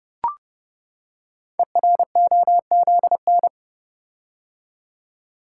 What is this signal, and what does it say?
Morse code, heard as a keyed tone, about 28 wpm, 700 Hz. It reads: EROZN
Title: Code Groups - 5 Letters - 30wpm